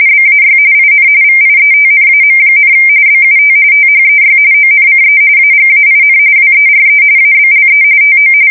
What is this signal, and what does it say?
RTTY, about 45.45 baud. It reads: BAY IN VIETNAM, BUT THEY ONCE AGAIN WERE ABLE TO
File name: fsk170 / battleshipnj